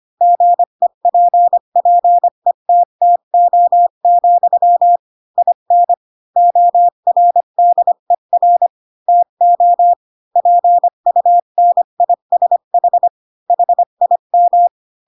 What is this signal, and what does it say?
Morse code, heard as a keyed tone, about 25 wpm, 700 Hz. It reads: GEPPETTO, IN ORDER TO PUNISH HIM
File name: Pinocchio / Pinocchio ch8 20 25 ik1hge